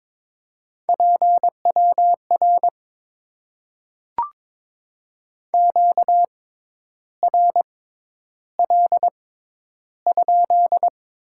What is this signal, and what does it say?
Morse code, heard as a keyed tone, about 22 wpm, spaced as for 10 wpm, 700 Hz. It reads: PWR QRL?
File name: Top 90 QSO Elements - 6x Character Spacing - 22wpm